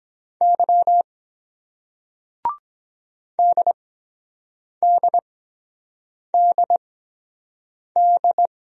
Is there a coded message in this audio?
Yes: Morse code from 0.4 to 8.5 s, heard as a keyed tone, about 23 wpm, spaced as for 7 wpm, 700 Hz. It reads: Y DDDD